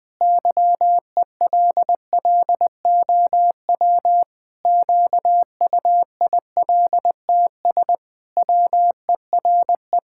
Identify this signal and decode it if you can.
Morse code, heard as a keyed tone, about 20 wpm, 700 Hz. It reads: YELLOW QUILTS WERE
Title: Pangram Challenge v2 - 20wpm